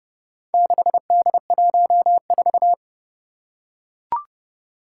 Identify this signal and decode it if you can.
Morse code, heard as a keyed tone, about 30 wpm, 700 Hz. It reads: 6D14
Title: Four Challenging Letter-Number Combinations 30wpm